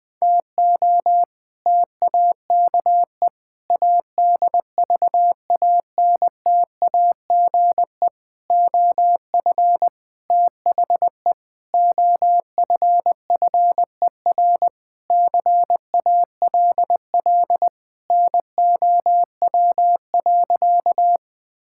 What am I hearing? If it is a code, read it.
Morse code, heard as a keyed tone, about 20 wpm, 700 Hz. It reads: TO TAKE ADVANTAGE OF THE OFFER CALL NOW.
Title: Sentences with 4-Word Phrases - 20wpm